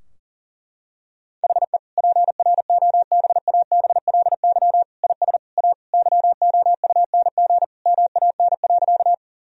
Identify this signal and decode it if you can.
Morse code, heard as a keyed tone, about 40 wpm, 700 Hz. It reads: HE PROBABLY IS A YOUNG MAN.
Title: Sentences from Top 1000 Words - Review Part 4 - 40wpm